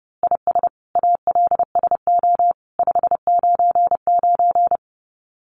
Morse code, heard as a keyed tone, about 30 wpm, 700 Hz. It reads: IS ALSO 599
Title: Common QSO Phrases 30wpm